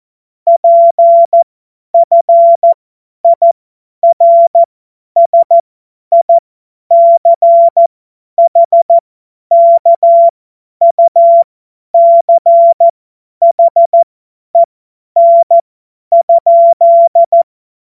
Morse code, heard as a keyed tone, about 14 wpm, spaced as for 10 wpm, 680 Hz. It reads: PFIRSICHKUCHEN?
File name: Morsekurs / Witze / Witz 06. April 2021, 50 F70